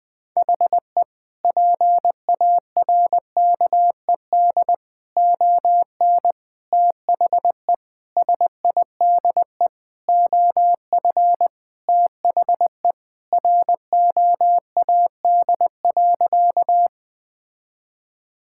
Morse code, heard as a keyed tone, about 20 wpm, 700 Hz. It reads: HE PARKED ON THE SIDE OF THE ROAD.
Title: Sentences with 5-Word Phrases - Spoken Beginning - 20wpm